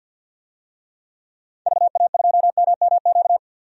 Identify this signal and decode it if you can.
Morse code, heard as a keyed tone, about 45 wpm, 700 Hz. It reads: VA2MMQ
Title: DXCC-100 Call Signs - 6 Characters - 50wpm